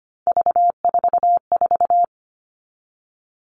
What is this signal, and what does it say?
Morse code, heard as a keyed tone, about 25 wpm, 700 Hz. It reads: V44
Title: All Cut Numbers - 3-Digit 25wpm